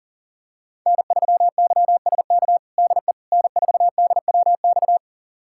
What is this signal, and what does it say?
Morse code, heard as a keyed tone, about 40 wpm, 700 Hz. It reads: N3YSK DE N4DWX
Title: USA Callsign DE Callsign - 5 Characters - 40wpm